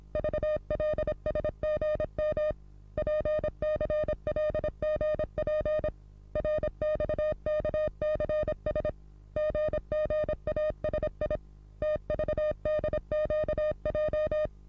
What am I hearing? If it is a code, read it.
Morse code, heard as a keyed tone, about 26 wpm, 610 Hz. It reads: VLSGM PCLGP RXKCS GGASI TVDQJ
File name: repos / Morsekurs / DJ4UF Morsen27